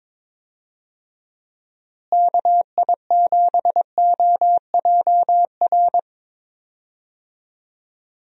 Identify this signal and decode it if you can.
Morse code, heard as a keyed tone, about 22 wpm, 700 Hz. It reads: KI7OJR
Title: USA Call Signs - 6 Characters - Repeated 2x - 22wpm